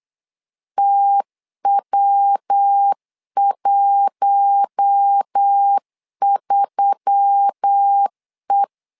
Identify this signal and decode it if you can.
Morse code, heard as a keyed tone, about 8 wpm, 800 Hz. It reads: TW13E
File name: LEZIONI / 4 Q1S3 8wpm 3037